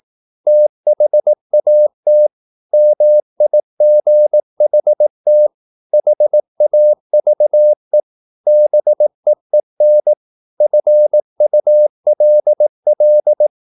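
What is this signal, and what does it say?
Morse code, heard as a keyed tone, about 18 wpm, 600 Hz. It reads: THAT MIGHT HAVE BEEN FULL
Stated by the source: Morse practice at 18 wpm, English text, from CW med Gnister.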